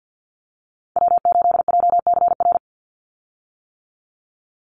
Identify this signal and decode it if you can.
Morse code, heard as a keyed tone, about 50 wpm, 700 Hz. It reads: W8JCR
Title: USA Call Signs 50wpm